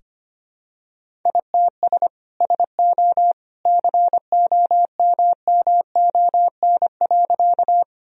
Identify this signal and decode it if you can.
Morse code, heard as a keyed tone, about 25 wpm, 700 Hz. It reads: ITS SO COMMON.
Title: Sentences from Top 1000 Words - Review Part 3 - 25wpm